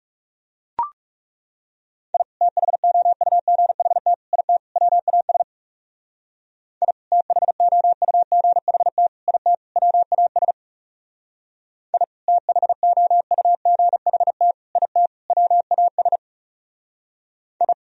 Morse code, heard as a keyed tone, about 35 wpm, 700 Hz. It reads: I THOUGHT IT WAS I THOUGHT IT WAS I THOUGHT IT WAS I
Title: Speed-Racing - Sets of 4 Words 30wpm